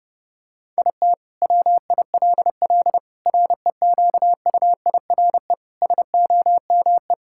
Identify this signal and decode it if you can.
Morse code, heard as a keyed tone, about 32 wpm, 700 Hz. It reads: IT WILL REQUIRE SOME
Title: Sentences from Top 1000 Words - Review Part 6 - 30wpm